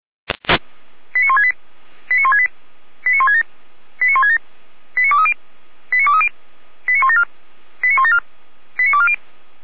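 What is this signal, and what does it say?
Five-tone selective call (ZVEI), heard as quick runs of five tones, tones of about 80 ms. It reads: 89168 89168 89167 89167 89230 89230 89174 89174 89240
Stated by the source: ZVEI five-tone selective call 89158 received with an SDR.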